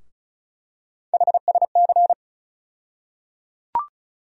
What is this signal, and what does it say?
Morse code, heard as a keyed tone, about 36 wpm, 700 Hz. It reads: HSC
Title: USA Call Sign Postfixes - 3 Letters 35wpm